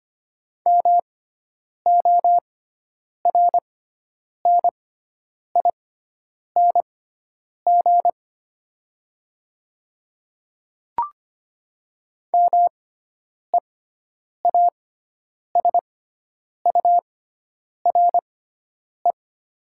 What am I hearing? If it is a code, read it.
Morse code, heard as a keyed tone, about 25 wpm, spaced as for 9 wpm, 700 Hz. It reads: MORNING MEASURE
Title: 6x Character Spacing - Top 500 Words - 25wpm